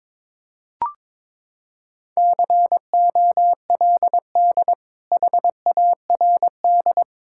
Morse code, heard as a keyed tone, about 22 wpm, 700 Hz. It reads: COLD HARD
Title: Sentences from Top 1000 Words - Review Part 5 - 22wpm